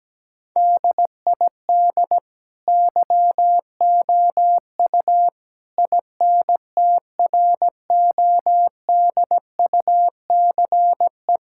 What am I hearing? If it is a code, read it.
Morse code, heard as a keyed tone, about 17 wpm, 700 Hz. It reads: DID YOU INTRODUCE
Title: Sentences from Top 1600 Words 17wpm